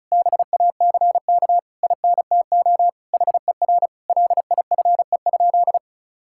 Morse code, heard as a keyed tone, about 34 wpm, 700 Hz. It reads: BACK INTO HER LIFE?
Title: Sentences from Top 1000 Words - Review Part 6 - 35wpm